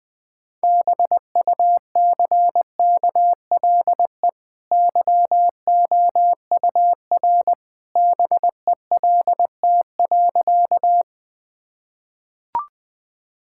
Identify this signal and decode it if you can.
Morse code, heard as a keyed tone, about 21 wpm, 700 Hz. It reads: BUCKLE YOUR BELT.
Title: Easy Sentences with Hints - 20wpm